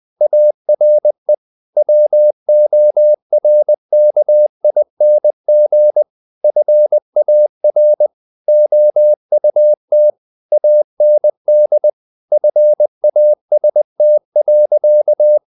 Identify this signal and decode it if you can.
Morse code, heard as a keyed tone, about 20 wpm, 600 Hz. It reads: ARE WORKING FAR OUT AND FAST.
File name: the old man and the sea / the old man and the sea part 1 20wpm